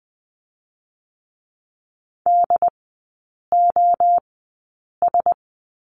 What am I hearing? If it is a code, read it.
Morse code, heard as a keyed tone, about 20 wpm, 700 Hz. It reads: D O S
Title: 114 Lesson 19e - Letter w - Three Characters 20wpm